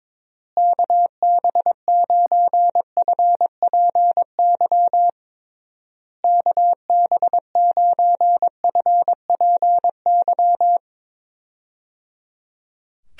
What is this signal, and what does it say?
Morse code, heard as a keyed tone, about 22 wpm, 700 Hz. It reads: KB9FPY KB9FPY
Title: USA Call Signs - 6 Characters - Repeated 3x - 22wpm